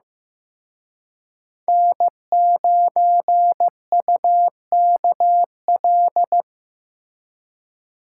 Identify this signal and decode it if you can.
Morse code, heard as a keyed tone, about 18 wpm, 700 Hz. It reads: N9UKL
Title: Speed Racing - USA 1x3 Call Signs 15wpm